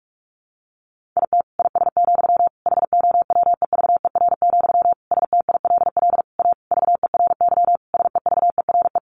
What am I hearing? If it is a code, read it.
Morse code, heard as a keyed tone, about 44 wpm, 700 Hz. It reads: IT IS, HOWEVER, STILL A VERY SEVERE
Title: Sentences from Top 1400 Words 45wpm